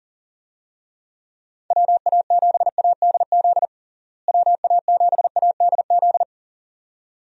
Morse code, heard as a keyed tone, about 37 wpm, 700 Hz. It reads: WA7ADZ WA7ADZ
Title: USA Call Signs - 6 Characters - Repeated 2x - 40wpm